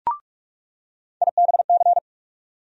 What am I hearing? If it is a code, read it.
Morse code, heard as a keyed tone, about 46 wpm, 700 Hz. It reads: IBC
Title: USA Call Sign Postfixes - 3 Letters 45wpm